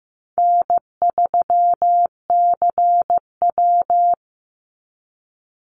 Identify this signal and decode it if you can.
Morse code, heard as a keyed tone, about 15 wpm, 700 Hz. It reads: N3CW
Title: Easy Call Signs - 15wpm